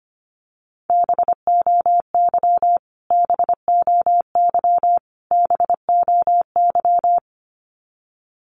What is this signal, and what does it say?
Morse code, heard as a keyed tone, about 25 wpm, 700 Hz. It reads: BOY BOY BOY
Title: Top 500 Words Repeated 25wpm